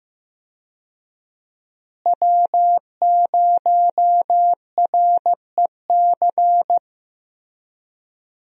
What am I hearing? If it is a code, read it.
Morse code, heard as a keyed tone, about 15 wpm, 700 Hz. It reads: W0REC
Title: USA Call Signs - 5 Characters - Repeated 3x - 15wpm